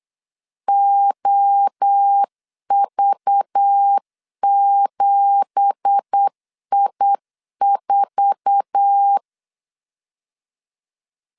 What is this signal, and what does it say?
Morse code, heard as a keyed tone, about 8 wpm, 800 Hz. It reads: OV7I4
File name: CW / DET / 1 9 1500C S A 8wpm 5637